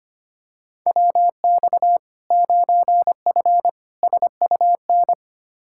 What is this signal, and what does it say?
Morse code, heard as a keyed tone, about 25 wpm, 700 Hz. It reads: WX 9F SUN
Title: Practice QSO - Exchange 2 - Info - 25wpm